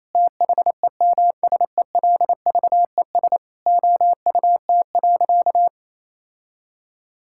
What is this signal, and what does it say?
Morse code, heard as a keyed tone, about 28 wpm, 700 Hz. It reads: THEMSELVES OUT.
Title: Sentences from Top 1000 Words - Review Part 5 - 28wpm